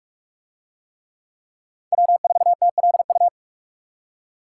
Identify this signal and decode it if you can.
Morse code, heard as a keyed tone, about 44 wpm, 700 Hz. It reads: W4TLU